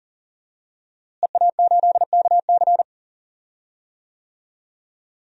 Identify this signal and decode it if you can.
Morse code, heard as a keyed tone, about 40 wpm, 700 Hz. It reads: EA8KC
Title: DXCC-100 Call Signs - 5 Characters - Repeated 2x - 40wpm